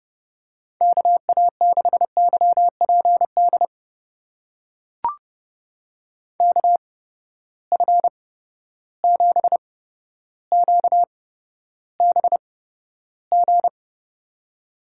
Morse code, heard as a keyed tone, about 30 wpm, spaced as for 13 wpm, 700 Hz. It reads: KA6YPD KF7QBG
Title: USA Call Signs - 6 Characters - 8x Character Spacing - 30wpm